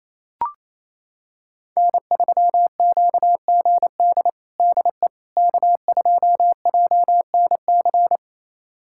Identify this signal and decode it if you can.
Morse code, heard as a keyed tone, about 28 wpm, 700 Hz. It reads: N3QGD DE K2JNC
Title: USA Callsign DE Callsign - 5 Characters - 28wpm